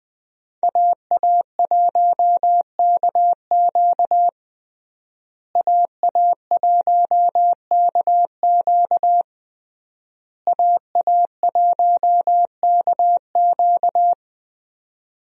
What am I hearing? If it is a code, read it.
Morse code, heard as a keyed tone, about 21 wpm, 700 Hz. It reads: AA1KQ AA1KQ AA1KQ
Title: USA Call Signs - 5 Characters - Repeated 3x - 20wpm